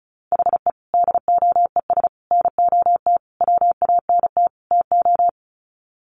Morse code, heard as a keyed tone, about 34 wpm, 700 Hz. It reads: HE DOES NOT WANT TO
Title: Sets of 5 Words 35wpm